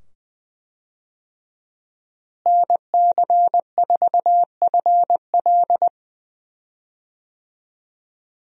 Morse code, heard as a keyed tone, about 21 wpm, 700 Hz. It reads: NC4FL
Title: USA Call Signs - 5 Characters - Repeated 2x - 20wpm